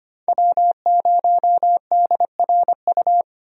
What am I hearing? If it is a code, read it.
Morse code, heard as a keyed tone, about 25 wpm, 700 Hz. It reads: W0DRU
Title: USA Call Signs - 5 Characters - 25wpm